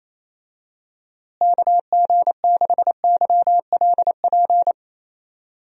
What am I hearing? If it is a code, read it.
Morse code, heard as a keyed tone, about 28 wpm, 700 Hz. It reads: KG6YLP
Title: USA Call Signs - 6 Characters - Repeated 3x - 28wpm